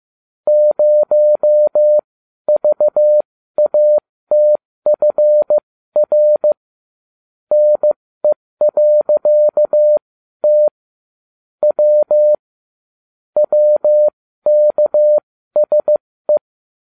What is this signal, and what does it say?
Morse code, heard as a keyed tone, about 15 wpm, spaced as for 12 wpm, 600 Hz. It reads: 0VATFR NE.T W WKSE